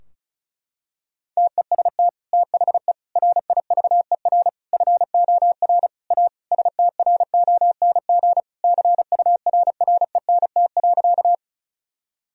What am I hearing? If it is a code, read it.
Morse code, heard as a keyed tone, about 36 wpm, 700 Hz. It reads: TEST THE RIVER FOR A STRONG CURRENT.
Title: Sentences from Top 1000 Words - Review Part 4 - 35wpm